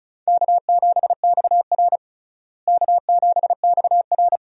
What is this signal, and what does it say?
Morse code, heard as a keyed tone, about 36 wpm, 700 Hz. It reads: K7XR K7XR
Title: USA Call Signs - 4 Characters - Repeated 2x - 35wpm